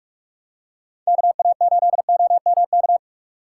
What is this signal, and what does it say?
Morse code, heard as a keyed tone, about 39 wpm, 700 Hz. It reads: KA8OMK